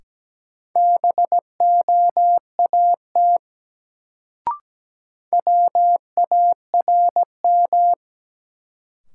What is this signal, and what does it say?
Morse code, heard as a keyed tone, about 17 wpm, 700 Hz. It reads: BOAT WARM
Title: Sentences from Top 500 Words - New Word Drill - 17wpm